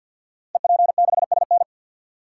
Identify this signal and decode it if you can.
Morse code, heard as a keyed tone, about 52 wpm, 700 Hz. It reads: EP6SN
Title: Code Groups - 5 Letter:Numbers - 50wpm